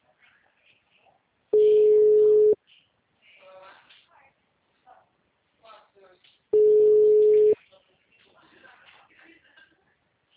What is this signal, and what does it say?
Ringback tone (Europe, 425 Hz), with speech.
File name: out- -2027- - - .154